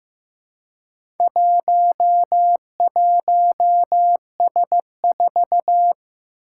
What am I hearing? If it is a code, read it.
Morse code, heard as a keyed tone, about 15 wpm, 700 Hz. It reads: 11S4
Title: Four Challenging Letter-Number Combinations 15wpm